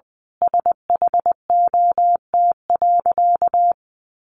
Morse code, heard as a keyed tone, about 20 wpm, 700 Hz. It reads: SHOT.
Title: A Sound Of Thunder 20wpm